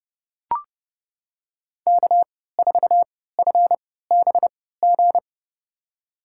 Morse code, heard as a keyed tone, about 30 wpm, spaced as for 17 wpm, 700 Hz. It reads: K4FBG
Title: USA Call Signs - 5 Characters - 3x Character Spacing - 30wpm